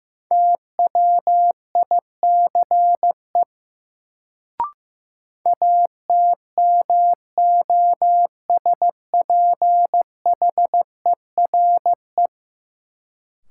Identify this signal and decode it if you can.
Morse code, heard as a keyed tone, about 15 wpm, 700 Hz. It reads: TWICE ATMOSPHERE
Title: Sentences from Top 1600 Words - New Word Drill - 15wpm